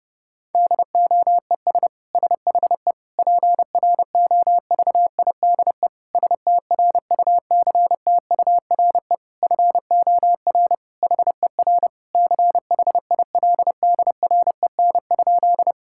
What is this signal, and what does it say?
Morse code, heard as a keyed tone, about 30 wpm, 700 Hz. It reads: DOES SHE PROVIDE STRUCTURE FOR HER CHILDREN?
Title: Sentences from Top 1000 Words - Review Part 6 - 30wpm